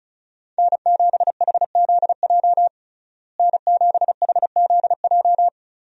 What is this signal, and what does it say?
Morse code, heard as a keyed tone, about 36 wpm, 700 Hz. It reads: N7HZJ N7HZJ
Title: USA Call Signs - 5 Characters - Repeated 3x - 35wpm